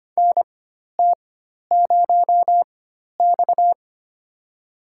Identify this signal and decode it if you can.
Morse code, heard as a keyed tone, about 25 wpm, spaced as for 12 wpm, 700 Hz. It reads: NT0X
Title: USA Call Signs - 4 Characters - 4x Character Spacing - 25wpm